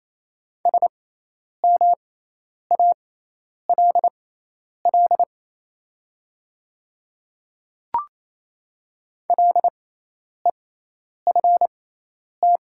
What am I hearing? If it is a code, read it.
Morse code, heard as a keyed tone, about 28 wpm, spaced as for 10 wpm, 700 Hz. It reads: SMALL LEFT
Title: 6x Character Spacing - Top 300 Words - 28wpm